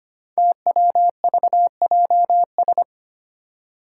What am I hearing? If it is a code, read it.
Morse code, heard as a keyed tone, about 25 wpm, 700 Hz. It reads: TWVJS
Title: Five Challenging Letter-Number Combinations 25wpm